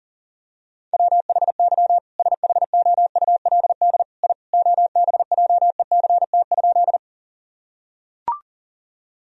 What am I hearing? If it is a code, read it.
Morse code, heard as a keyed tone, about 40 wpm, 700 Hz. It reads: WHY SHOULD I OBJECT?
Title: Sentences from Top 1000 Words - Review Part 3 - 40wpm